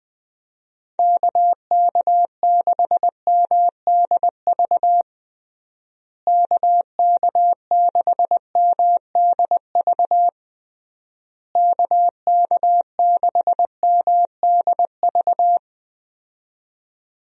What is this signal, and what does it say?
Morse code, heard as a keyed tone, about 20 wpm, 700 Hz. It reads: KK6MDV KK6MDV KK6MDV
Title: USA Call Signs - 6 Characters - Repeated 3x - 20wpm